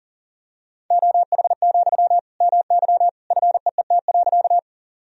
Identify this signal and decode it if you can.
Morse code, heard as a keyed tone, about 40 wpm, 700 Hz. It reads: OH, MY FEET.
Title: Sentences from Top 1000 Words - Review Part 1 - 40wpm